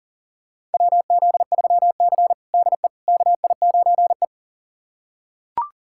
Morse code, heard as a keyed tone, about 37 wpm, 700 Hz. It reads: WZ3C DE KI9E
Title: USA Callsign DE Callsign - 4 Characters - 40wpm